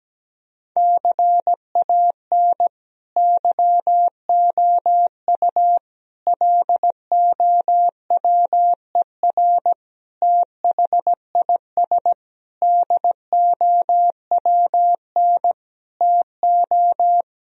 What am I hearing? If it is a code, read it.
Morse code, heard as a keyed tone, about 17 wpm, 700 Hz. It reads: CAN YOU LOWER THIS DOWN TO
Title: Sentences from Top 1000 Words - Review Part 4 - 17wpm